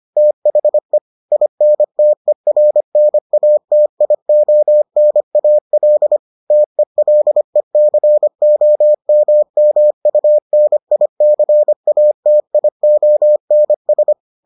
Morse code, heard as a keyed tone, about 25 wpm, 600 Hz. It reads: THE INTERNATIONAL TELECOMMUNICATIONS